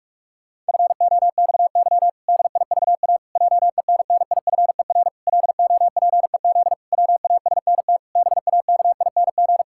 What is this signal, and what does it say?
Morse code, heard as a keyed tone, about 46 wpm, 700 Hz. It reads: FOXY DIVA JENNIFER LOPEZ WASNT BAKING
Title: Pangram Challenge v2 - 45wpm